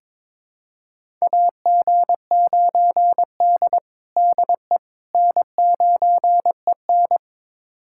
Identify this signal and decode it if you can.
Morse code, heard as a keyed tone, about 22 wpm, 700 Hz. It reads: AG9D DE N9EN
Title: USA Callsign DE Callsign - 4 Characters - 22wpm